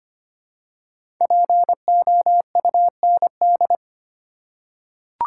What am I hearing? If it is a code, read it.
Morse code, heard as a keyed tone, about 25 wpm, 700 Hz. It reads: POUND
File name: 5 Letter Words - First 3 Letters Given - 25wpm